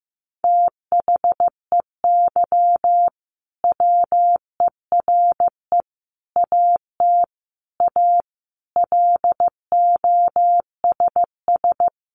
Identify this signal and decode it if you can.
Morse code, heard as a keyed tone, about 14 wpm, 700 Hz. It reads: THEY WERE AT A LOSS
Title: Sentences from Top 1400 Words 15wpm